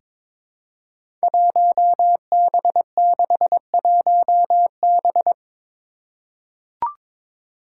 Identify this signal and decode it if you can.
Morse code, heard as a keyed tone, about 22 wpm, 700 Hz. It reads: 1B61B
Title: Five Challenging Letter-Number Combinations 22wpm